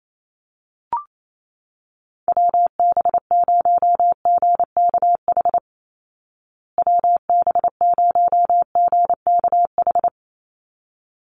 Morse code, heard as a keyed tone, about 28 wpm, 700 Hz. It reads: WB0GKH WB0GKH
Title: Call Signs Repeated 3x - 28wpm